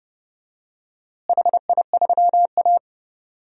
Morse code, heard as a keyed tone, about 30 wpm, 700 Hz. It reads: HI3A
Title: DXCC-100 Call Signs - 4 Characters - Repeated 2x - 30wpm